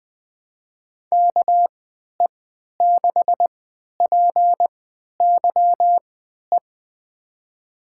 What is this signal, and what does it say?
Morse code, heard as a keyed tone, about 20 wpm, spaced as for 11 wpm, 700 Hz. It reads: KE6PYE
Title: USA Call Signs - 6 Characters - 3x Character Spacing - 20wpm